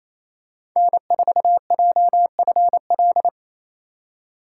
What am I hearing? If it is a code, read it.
Morse code, heard as a keyed tone, about 28 wpm, 700 Hz. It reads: N4JFL
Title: USA Call Signs - 5 Characters - 4x Character Spacing - 28wpm